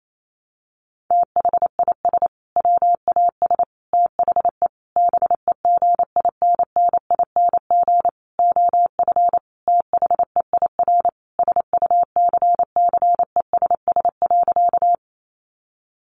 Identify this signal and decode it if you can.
Morse code, heard as a keyed tone, about 28 wpm, 690 Hz. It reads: THIS WAS THE BEGINNING OF THEIR SUCCESS.
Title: Sentences from Top 900 Words 28wpm